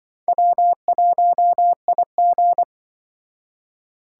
Morse code, heard as a keyed tone, about 24 wpm, 700 Hz. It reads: W1IG
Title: Speed Racing - USA 1x2 Call Signs 20wpm